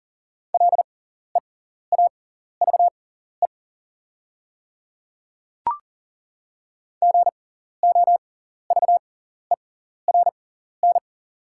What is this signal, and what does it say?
Morse code, heard as a keyed tone, about 41 wpm, spaced as for 14 wpm, 700 Hz. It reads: LEAVE GOVERN
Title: 6x Character Spacing - Top 500 Words - 40wpm